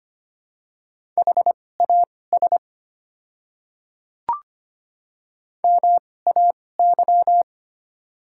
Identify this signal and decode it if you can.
Morse code, heard as a keyed tone, about 25 wpm, spaced as for 18 wpm, 700 Hz. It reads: HAS MAY
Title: 2x Character Spacing - Top 100 Words - 25wpm